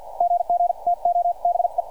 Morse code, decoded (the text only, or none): GG TO BE